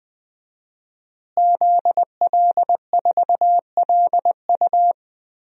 Morse code, heard as a keyed tone, about 21 wpm, 700 Hz. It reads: ZL4LU
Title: DXCC-100 Call Signs - 5 Characters - Repeated 3x - 20wpm